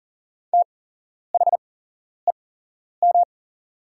Morse code, heard as a keyed tone, about 39 wpm, spaced as for 11 wpm, 690 Hz. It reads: THEM
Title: 8x Character Spacing - Top 100 Words - 40wpm